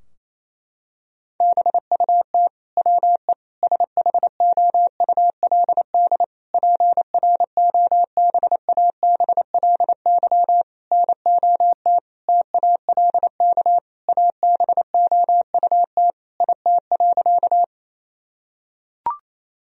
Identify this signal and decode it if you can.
Morse code, heard as a keyed tone, about 28 wpm, 700 Hz. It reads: BUT WE SHOULD PROBABLY NOT TALK ABOUT IT.
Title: Sentences from Top 1000 Words - Review Part 4 - 28wpm